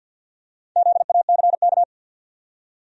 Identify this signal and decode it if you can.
Morse code, heard as a keyed tone, about 50 wpm, 700 Hz. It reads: ZACK
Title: USA Common Names - Max 5 Letters - 50wpm